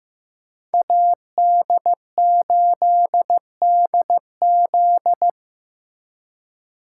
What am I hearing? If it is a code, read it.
Morse code, heard as a keyed tone, about 18 wpm, 700 Hz. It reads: AD8DZ
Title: Speed Racing - USA 2x2 Call Signs 15wpm